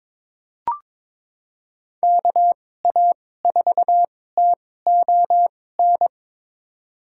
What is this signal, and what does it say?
Morse code, heard as a keyed tone, about 22 wpm, spaced as for 16 wpm, 700 Hz. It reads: KA4TON
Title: USA Call Signs - 6 Characters - 2x Character Spacing - 22wpm